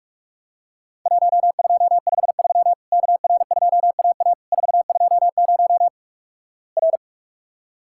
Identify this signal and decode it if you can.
Morse code, heard as a keyed tone, about 48 wpm, 700 Hz. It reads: 1253 KR2AA 420 R
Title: Worked All Europe DX Contest - QTC Exchange - 45wpm